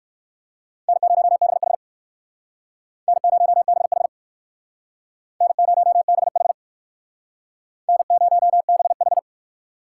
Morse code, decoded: N0BH N0BH N0BH N0BH

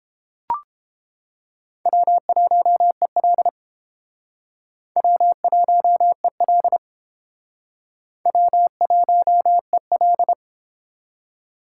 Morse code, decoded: W1EL W1EL W1EL

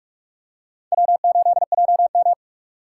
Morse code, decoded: W8JM

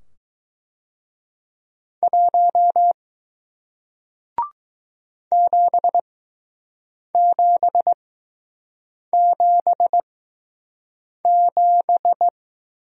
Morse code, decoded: 1 7777

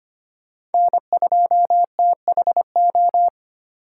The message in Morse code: N2THO

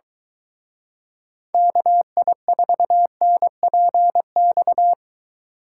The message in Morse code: KI4NPX

